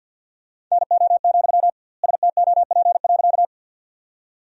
Morse code, heard as a keyed tone, about 50 wpm, 700 Hz. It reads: NO, STOP.
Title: 2x Word-Spacing - Sentences from Top 300 Words - 50wpm